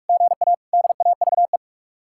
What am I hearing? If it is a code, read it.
Morse code, heard as a keyed tone, about 59 wpm, 700 Hz. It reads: GA DAVE